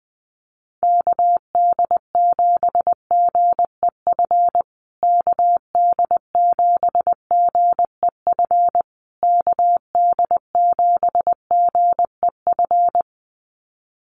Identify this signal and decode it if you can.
Morse code, heard as a keyed tone, about 20 wpm, 700 Hz. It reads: KD7GEF KD7GEF KD7GEF
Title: USA Call Signs Repeated 3x 20wpm